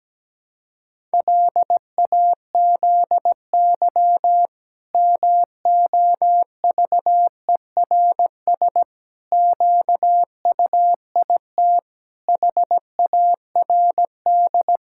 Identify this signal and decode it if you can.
Morse code, heard as a keyed tone, about 17 wpm, 700 Hz. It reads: LAZY MOVERS QUIT HARD
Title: Pangram Challenge v2 - 17wpm